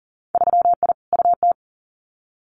Morse code, heard as a keyed tone, about 40 wpm, 700 Hz. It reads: 3I UT
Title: Winter Field Day Exchanges Encore 40wpm